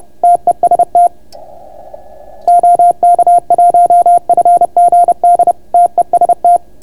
Morse code, heard as a keyed tone, about 31 wpm, 690 Hz. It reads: TEST OK1FGD TEST